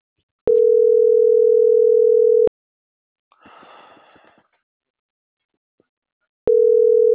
Ringback tone (North America), 440 + 480 Hz.